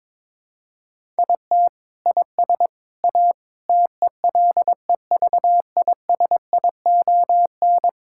Morse code, heard as a keyed tone, about 22 wpm, 700 Hz. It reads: IT IS A TELEVISION
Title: Sentences from Top 1000 Words - Review Part 5 - 22wpm